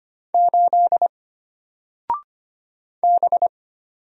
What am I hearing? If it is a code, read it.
Morse code, heard as a keyed tone, about 25 wpm, 700 Hz. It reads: 8 B
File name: Single-Letter-Number-25wpm